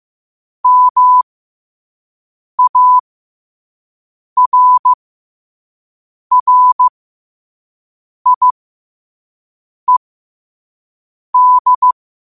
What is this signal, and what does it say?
Morse code, heard as a keyed tone, about 15 wpm, spaced as for 5 wpm, 1000 Hz. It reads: MARRIED
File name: qotdfiles / 5wpm